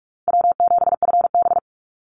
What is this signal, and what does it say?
Morse code, heard as a keyed tone, about 44 wpm, 700 Hz. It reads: W7FB
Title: Call Signs - 45wpm